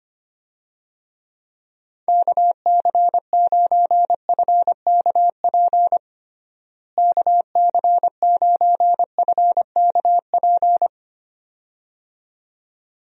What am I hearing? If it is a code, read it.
Morse code, heard as a keyed tone, about 25 wpm, 700 Hz. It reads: KC9FKP KC9FKP
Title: USA Call Signs - 6 Characters - Repeated 2x - 25wpm